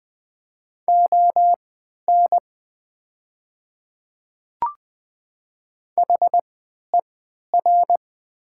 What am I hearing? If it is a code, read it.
Morse code, heard as a keyed tone, about 20 wpm, spaced as for 11 wpm, 700 Hz. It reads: ON HER
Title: 3x Character Spacing - Top 300 Words - 20wpm